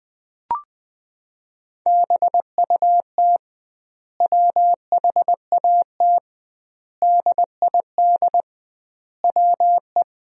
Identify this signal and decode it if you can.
Morse code, heard as a keyed tone, about 19 wpm, 700 Hz. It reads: BUT WHAT DID WE
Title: 2x Word-Spacing - Sentences from Top 400 Words - 20wpm